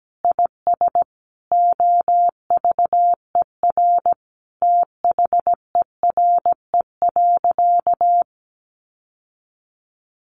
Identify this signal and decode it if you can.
Morse code, heard as a keyed tone, about 17 wpm, 700 Hz. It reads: IS OVER THERE.
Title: Sentences from Top 1100 Words 17wpm